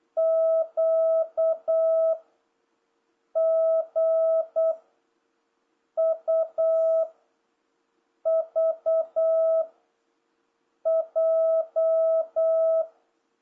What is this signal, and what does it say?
Morse code, heard as a keyed tone, about 8 wpm, 650 Hz. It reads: Q G U V J